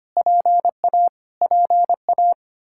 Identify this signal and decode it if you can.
Morse code, heard as a keyed tone, about 25 wpm, 700 Hz. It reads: PA PA
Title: Practice QSO - Exchange 1 - Beginning - 25wpm